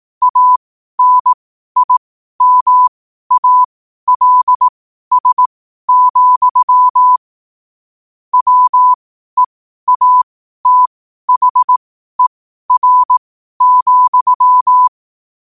Morse code, heard as a keyed tone, about 18 wpm, spaced as for 12 wpm, 1000 Hz. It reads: ANIMALS, WEATHER,
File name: qotdfiles / 12wpm